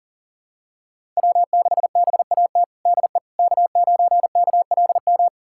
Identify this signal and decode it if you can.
Morse code, heard as a keyed tone, about 37 wpm, 700 Hz. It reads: W6BAT DE K9KLM
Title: USA Callsign DE Callsign - 5 Characters - 40wpm